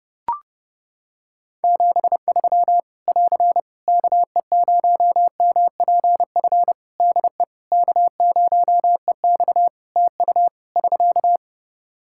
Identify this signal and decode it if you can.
Morse code, heard as a keyed tone, about 30 wpm, 700 Hz. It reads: 73 <AR> KE0MPF DE K0EX TU <SK>
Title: Practice QSO - Exchange 4 - Ending - 30wpm